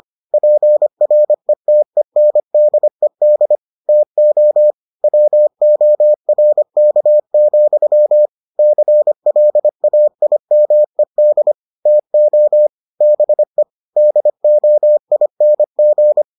Morse code, heard as a keyed tone, about 25 wpm, 600 Hz. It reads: PRETENDED TO WORK, CLAIMED TO BE DOING